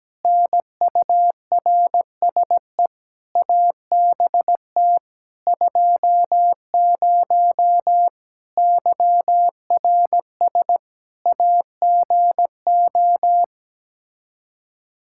Morse code, heard as a keyed tone, about 17 wpm, 700 Hz. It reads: NURSE ABT 20 YRS AGO
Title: Practice QSO - Exchange 3 - More Info - 17wpm